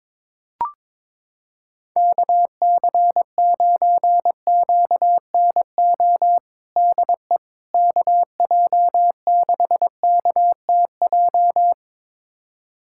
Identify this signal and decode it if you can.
Morse code, heard as a keyed tone, about 22 wpm, 700 Hz. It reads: KC9QNO DE KJ6KTJ
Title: USA Callsign DE Callsign 22wpm